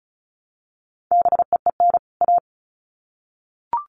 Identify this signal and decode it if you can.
Morse code, heard as a keyed tone, about 34 wpm, 700 Hz. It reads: BEEN A
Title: Sets of 2 Words 35wpm